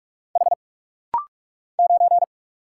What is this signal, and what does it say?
Morse code, heard as a keyed tone, about 45 wpm, 700 Hz. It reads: H 9
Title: Single Letter-Number Alt - Rapid-Fire 45wpm